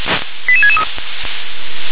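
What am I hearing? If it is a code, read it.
Five-tone selective call (ZVEI), heard as quick runs of five tones, tones of about 70 ms. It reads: 88663
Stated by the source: ZVEI five-tone call 88704 received by SDR.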